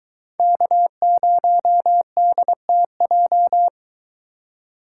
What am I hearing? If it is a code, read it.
Morse code, heard as a keyed tone, about 22 wpm, 700 Hz. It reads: K0DTJ